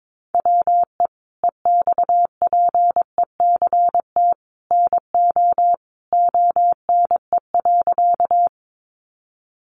Morse code, heard as a keyed tone, about 22 wpm, 700 Hz. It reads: WE EXPECT NO ONE.
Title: Sentences from Top 1100 Words 22wpm